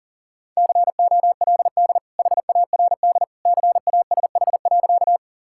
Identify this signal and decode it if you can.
Morse code, heard as a keyed tone, about 40 wpm, 700 Hz. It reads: COLD HARD CASH.
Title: Sentences from Top 1000 Words - Review Part 5 - 40wpm